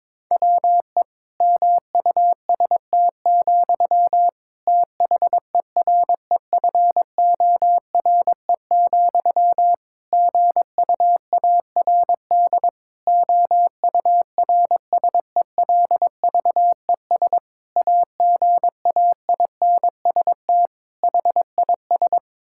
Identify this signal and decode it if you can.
Morse code, heard as a keyed tone, about 22 wpm, 700 Hz. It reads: WE MUST, THEREFORE, GUARD OURSELVES AGAINST HIS
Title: Sentences from Top 1600 Words 22wpm